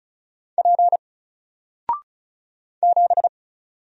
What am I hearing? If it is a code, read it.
Morse code, heard as a keyed tone, about 36 wpm, 700 Hz. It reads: P 7
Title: Single Character (Letter:Number:Punctuation:Prosign) Alt 35wpm